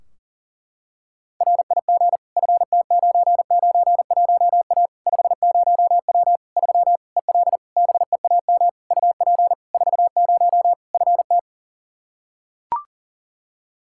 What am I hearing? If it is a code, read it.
Morse code, heard as a keyed tone, about 40 wpm, 700 Hz. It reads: RIG FT991A 50W 3 EL BEAM UP 40 FT
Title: Practice QSO - Exchange 2 - Info - 40wpm